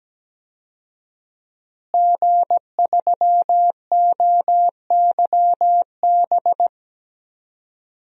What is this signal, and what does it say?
Morse code, heard as a keyed tone, about 17 wpm, 700 Hz. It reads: G3OYB
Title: DXCC-100 Call Signs - 5 Characters - Repeated 2x - 17wpm